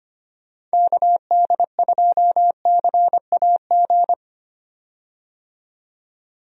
Morse code, heard as a keyed tone, about 25 wpm, 700 Hz. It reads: KD2CAG
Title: USA Call Signs - 6 Characters - 25wpm